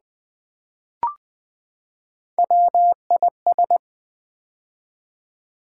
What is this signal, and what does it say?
Morse code, heard as a keyed tone, about 20 wpm, 700 Hz. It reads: WIS
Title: Build A Word - 3 Letters At A Time - 6-Letter Words - 20wpm